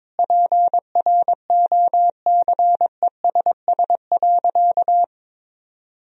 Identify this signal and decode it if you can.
Morse code, heard as a keyed tone, about 21 wpm, 700 Hz. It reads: PROCESS.
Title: Sentences from Top 1000 Words - Review Part 4 - 22wpm